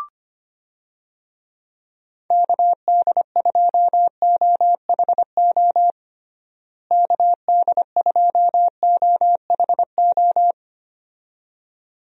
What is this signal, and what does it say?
Morse code, heard as a keyed tone, about 25 wpm, 700 Hz. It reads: KD2OHO KD2OHO
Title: USA Call Signs - 6 Characters - Repeated 2x - 25wpm